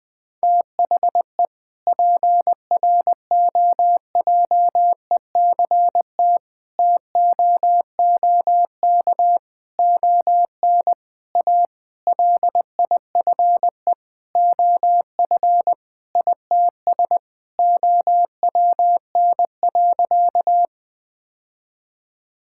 Morse code, heard as a keyed tone, about 21 wpm, 700 Hz. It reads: THE PROJECT TOOK ON A LIFE OF ITS OWN.
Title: Sentences with 7-Word Phrases - 20wpm